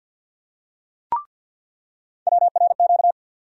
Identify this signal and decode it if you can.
Morse code, heard as a keyed tone, about 53 wpm, 700 Hz. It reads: WRQ
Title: USA Call Sign Postfixes - 3 Letters 50wpm